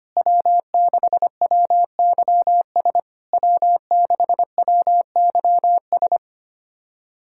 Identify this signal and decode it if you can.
Morse code, heard as a keyed tone, about 24 wpm, 700 Hz. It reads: W6WYS W6WYS
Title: SOTA QSO Practice 25wpm